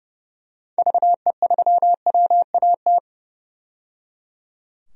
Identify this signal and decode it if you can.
Morse code, heard as a keyed tone, about 30 wpm, 700 Hz. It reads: VE3WAT
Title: DXCC-100 Call Signs - 6 Characters - 30wpm